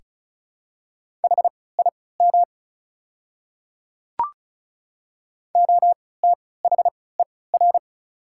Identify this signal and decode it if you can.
Morse code, heard as a keyed tone, about 36 wpm, spaced as for 20 wpm, 690 Hz. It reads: HIM OTHER
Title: 3x Character Spacing - Top 100 Words - 35wpm